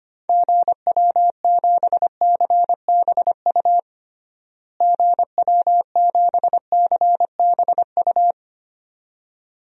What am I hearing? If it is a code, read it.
Morse code, heard as a keyed tone, about 25 wpm, 700 Hz. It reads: GW7CBU GW7CBU
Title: DXCC-100 Call Signs - 6 Characters - Repeated 2x - 25wpm